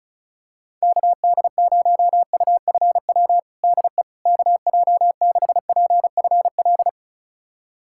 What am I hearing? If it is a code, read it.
Morse code, heard as a keyed tone, about 34 wpm, 690 Hz. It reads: KD0UFW DE KJ6PFL
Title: USA Callsign DE Callsign - 6 Characters - 35wpm